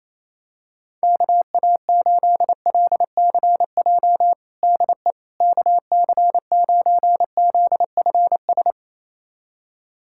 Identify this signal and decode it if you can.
Morse code, heard as a keyed tone, about 27 wpm, 700 Hz. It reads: KA8LCJ DE KC9ZFS
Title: USA Callsign DE Callsign - 6 Characters - 28wpm